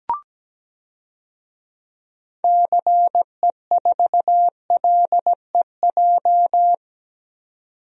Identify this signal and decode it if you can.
Morse code, heard as a keyed tone, about 17 wpm, 700 Hz. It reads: CE4LEJ
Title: DXCC-100 Call Signs - 6 Characters - 17wpm